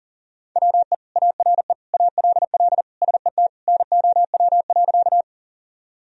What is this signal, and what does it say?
Morse code, heard as a keyed tone, about 40 wpm, 700 Hz. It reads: WE ARE ALL SET NOW.
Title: Sentences from Top 1000 Words - Review Part 1 - 40wpm